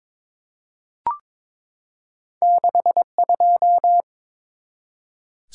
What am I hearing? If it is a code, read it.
Morse code, heard as a keyed tone, about 22 wpm, 700 Hz. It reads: 62
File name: practice / 22wpm / Two Letter-Number Combinations - 22wpm